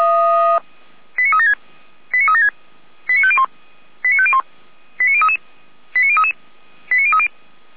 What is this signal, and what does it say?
Five-tone selective call (ZVEI), heard as quick runs of five tones, tones of about 70 ms. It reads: 89376 89376 89501 89501 89033 89033 89030